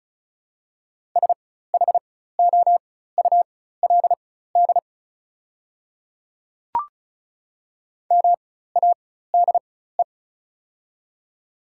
Morse code, heard as a keyed tone, about 36 wpm, spaced as for 16 wpm, 700 Hz. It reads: SHOULD MADE
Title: 4x Character Spacing - Top 300 Words - 35wpm